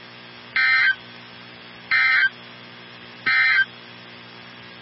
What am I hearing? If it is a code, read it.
Emergency Alert System (SAME) tones, heard as harsh buzzing bursts. It is heard from 0.4 to 3.7 s.
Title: NOAA Weather Radio 162.550